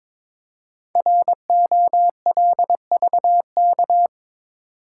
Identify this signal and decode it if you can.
Morse code, heard as a keyed tone, about 22 wpm, 690 Hz. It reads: ROLVK
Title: Code Groups - 5 Letter:Numbers - 22wpm